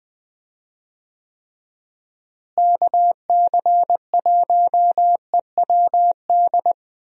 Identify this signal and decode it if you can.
Morse code, heard as a keyed tone, about 20 wpm, 700 Hz. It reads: KC1EWD
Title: USA Call Signs - 6 Characters - Repeated 2x - 20wpm